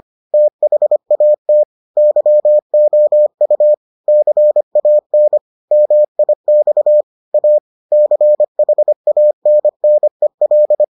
Morse code, heard as a keyed tone, about 25 wpm, 600 Hz. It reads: THAT YOU CAN MIX A CHANNEL